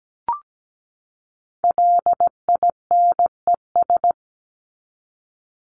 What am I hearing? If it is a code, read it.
Morse code, heard as a keyed tone, about 17 wpm, 700 Hz. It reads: LINES
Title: ES:S Suffix - Top 100 Words with Base Words 17wpm